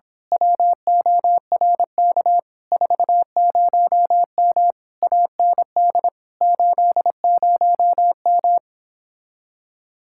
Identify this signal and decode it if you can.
Morse code, heard as a keyed tone, about 26 wpm, 700 Hz. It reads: WORK 40M AND 80M
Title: Speed-Racing Common QSO Phrases - 22wpm